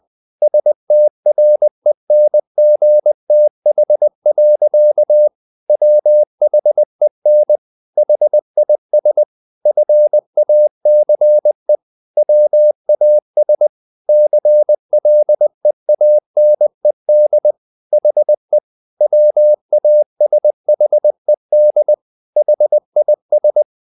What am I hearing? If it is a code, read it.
Morse code, heard as a keyed tone, about 20 wpm, 600 Hz. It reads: STRENGTH. WHEN HIS FACE WAS CLEANED HE WASHED HIS
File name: the old man and the sea part 2 20wpm